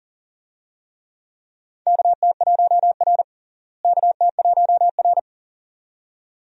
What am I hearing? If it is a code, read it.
Morse code, heard as a keyed tone, about 37 wpm, 700 Hz. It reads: KT1R KT1R
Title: USA Call Signs - 4 Characters - Repeated 2x - 40wpm